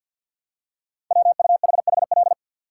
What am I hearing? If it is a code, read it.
Morse code, heard as a keyed tone, about 48 wpm, 700 Hz. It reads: WUHHL
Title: Code Groups - 5 Letters - 50wpm